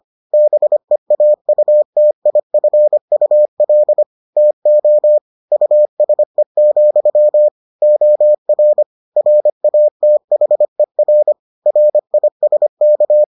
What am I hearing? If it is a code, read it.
Morse code, heard as a keyed tone, about 25 wpm, 600 Hz. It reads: BEAUTIFUL TO USE, OR RATHER RISK